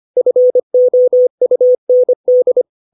Morse code, heard as a keyed tone, about 25 wpm, 500 Hz. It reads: FOUND